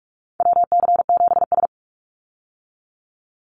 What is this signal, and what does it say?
Morse code, heard as a keyed tone, about 44 wpm, 700 Hz. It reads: WC7S
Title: Call Signs - 45wpm